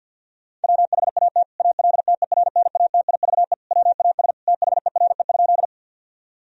Morse code, heard as a keyed tone, about 50 wpm, 700 Hz. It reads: WHAT ALTERNATIVE WAS THERE?
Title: Sentences from Top 1000 Words - Review Part 4 - 50wpm